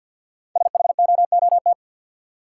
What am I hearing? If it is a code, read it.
Morse code, heard as a keyed tone, about 52 wpm, 700 Hz. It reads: SHOOT
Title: Sentences from Top 1200 Words - New Word Drill Alt - 50wpm